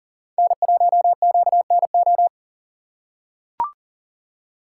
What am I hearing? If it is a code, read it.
Morse code, heard as a keyed tone, about 40 wpm, 700 Hz. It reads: N1QNO
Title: USA Call Signs - 5 Characters - 2x Character Spacing - 40wpm